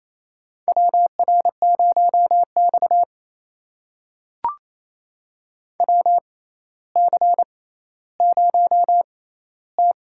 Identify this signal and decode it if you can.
Morse code, heard as a keyed tone, about 28 wpm, spaced as for 14 wpm, 700 Hz. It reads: WR0X WC0T
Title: USA Call Signs - 4 Characters - 6x Character Spacing - 28wpm